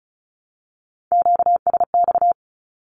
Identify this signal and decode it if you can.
Morse code, heard as a keyed tone, about 34 wpm, 700 Hz. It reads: QSX
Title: Q-Codes Repeated 3x 35wpm